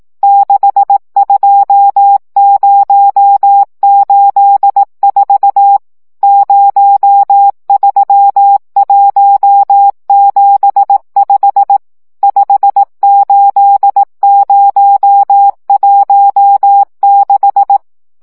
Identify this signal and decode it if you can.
Morse code, heard as a keyed tone, about 18 wpm, 800 Hz. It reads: 62084 03175 58016
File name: morze vizsga 2011-1 15